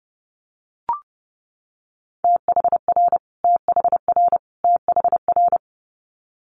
Morse code, heard as a keyed tone, about 30 wpm, 700 Hz. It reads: THR THR THR
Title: Multi-letter Phonemes Repeated 3x 30wpm